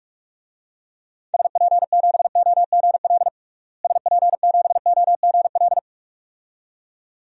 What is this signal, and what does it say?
Morse code, heard as a keyed tone, about 46 wpm, 700 Hz. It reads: SP7OGL SP7OGL
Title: DXCC-100 Call Signs - 6 Characters - Repeated 2x - 45wpm